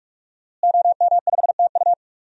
Morse code, heard as a keyed tone, about 47 wpm, 700 Hz. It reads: OM5TU
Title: DXCC-100 Call Signs - 5 Characters - Repeated 3x - 45wpm